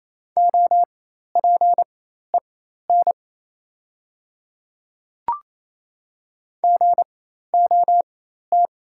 Morse code, heard as a keyed tone, about 28 wpm, spaced as for 13 wpm, 690 Hz. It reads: OPEN GOT
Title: 4x Character Spacing - Top 300 Words - 28wpm